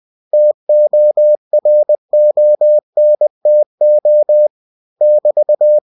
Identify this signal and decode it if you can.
Morse code, heard as a keyed tone, about 20 wpm, 600 Hz. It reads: TORONTO <BT>
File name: cw-books / winnie-the-pooh / winnie-20wpm-page-0111